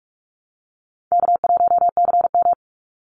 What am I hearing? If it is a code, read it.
Morse code, heard as a keyed tone, about 46 wpm, 690 Hz. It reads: K1CM
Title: NA SOTA Chaser Callsigns 45wpm